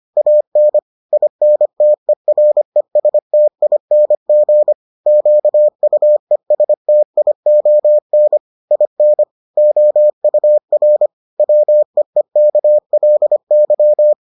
Morse code, heard as a keyed tone, about 25 wpm, 600 Hz. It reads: AN INTERESTING QUESTION IN OUR WEEKLY